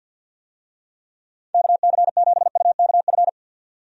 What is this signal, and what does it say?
Morse code, heard as a keyed tone, about 48 wpm, 700 Hz. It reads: KC7UKF
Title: USA Call Signs - 6 Characters - 50wpm